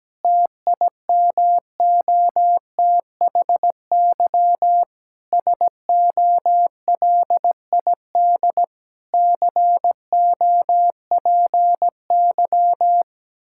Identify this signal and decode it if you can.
Morse code, heard as a keyed tone, about 17 wpm, 700 Hz. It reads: TIMOTHY SOLID COPY
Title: K1USN SST QSO Practice 17wpm